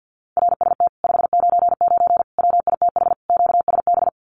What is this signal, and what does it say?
Morse code, heard as a keyed tone, about 48 wpm, 700 Hz. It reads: RST 599 WITH QSB
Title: Common QSO Phrases Encore 50wpm